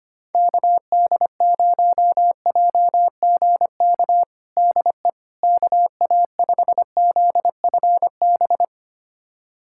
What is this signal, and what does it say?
Morse code, heard as a keyed tone, about 25 wpm, 700 Hz. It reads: KD0JGK DE KA5ZFB
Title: USA Callsign DE Callsign - 6 Characters - 25wpm